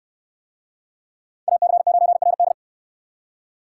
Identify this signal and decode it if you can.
Morse code, heard as a keyed tone, about 64 wpm, 700 Hz. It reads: AC9RD